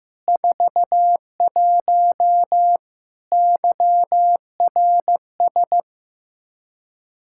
Morse code, heard as a keyed tone, about 15 wpm, 700 Hz. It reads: 41 YRS
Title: Practice QSO - Exchange 3 - More Info - 15wpm